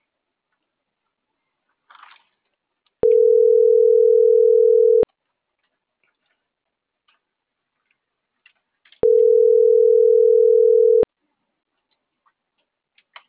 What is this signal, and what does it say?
Ringback tone (North America), 440 + 480 Hz.